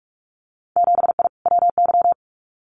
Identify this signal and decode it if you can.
Morse code, heard as a keyed tone, about 46 wpm, 700 Hz. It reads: 7I WY
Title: Winter Field Day Exchanges Encore 45wpm